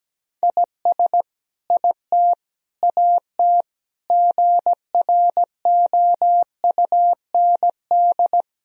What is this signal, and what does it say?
Morse code, heard as a keyed tone, about 17 wpm, 700 Hz. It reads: IS IT AT GROUND
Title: Sentences from Top 1000 Words - Review Part 4 - 17wpm